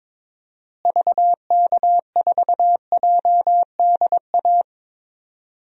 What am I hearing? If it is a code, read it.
Morse code, heard as a keyed tone, about 22 wpm, 700 Hz. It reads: VK4JDA
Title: DXCC-100 Call Signs - 6 Characters - Repeated 3x - 22wpm